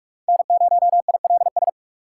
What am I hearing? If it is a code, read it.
Morse code, heard as a keyed tone, about 45 wpm, 700 Hz. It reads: N0ILS
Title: USA Call Signs - 6x Character Spacing - 45wpm